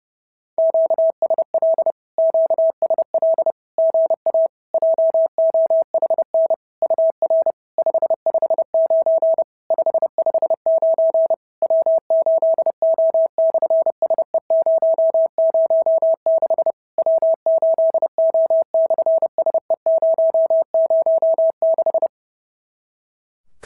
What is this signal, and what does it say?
Morse code, heard as a keyed tone, about 33 wpm, 650 Hz. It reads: QSL QSL GA JOHN UR 559 559 W8O/SE006 W8O/SE006
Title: SOTA QSO Practice 30wpm